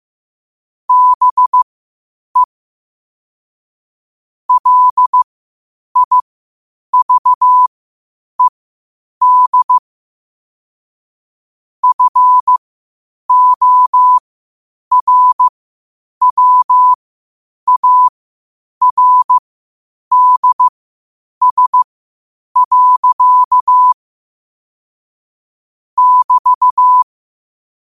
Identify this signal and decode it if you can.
Morse code, heard as a keyed tone, about 15 wpm, spaced as for 8 wpm, 1000 Hz. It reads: BE LIVED FORWARDS. <BT>
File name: qotdfiles / 8wpm